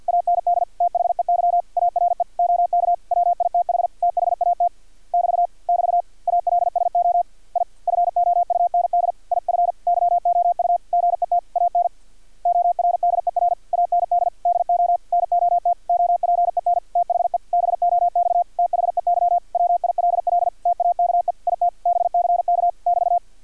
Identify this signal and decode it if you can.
Morse code, heard as a keyed tone, about 50 wpm, 700 Hz, with radio noise. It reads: AND THEY ARE OK WITH THAT <BT> <BT> ALSO I FOUND IF YOU GET AN ORDER AND DO NOT OPEN THE BOX THEY WILL TAKE IT BCK <BT>
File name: archived / CRSnet